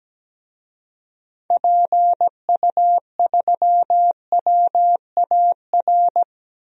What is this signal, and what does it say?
Morse code, heard as a keyed tone, about 17 wpm, 700 Hz. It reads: PU3WAR
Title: DXCC-100 Call Signs - 6 Characters - 17wpm